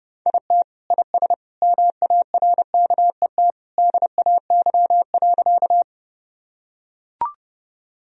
Morse code, heard as a keyed tone, about 29 wpm, 700 Hz. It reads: IT IS MARKET DAY.
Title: Sentences from Top 1000 Words - Review Part 3 - 30wpm